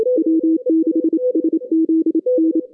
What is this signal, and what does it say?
Morse code, heard as a keyed tone, about 27 wpm, 310 Hz, with radio noise. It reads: W6SZN
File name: 2015 cq ww cw / 2015 cqwww k5tr - 0-18